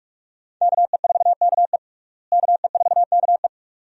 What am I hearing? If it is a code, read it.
Morse code, heard as a keyed tone, about 42 wpm, 700 Hz. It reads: KE4KE KE4KE
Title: USA Call Signs - 5 Characters - Repeated 2x - 45wpm